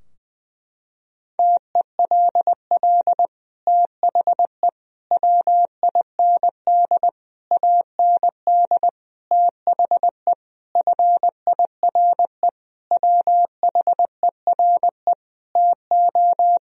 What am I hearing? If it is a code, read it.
Morse code, heard as a keyed tone, about 20 wpm, 700 Hz. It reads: TELL THE WIND AND THE FIRE WHERE TO
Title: Sentences from Top 1000 Words - Review Part 2 - 20wpm